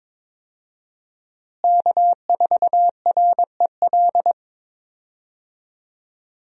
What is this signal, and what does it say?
Morse code, heard as a keyed tone, about 22 wpm, 700 Hz. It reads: K4REL
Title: USA Call Signs - 5 Characters - Repeated 3x - 22wpm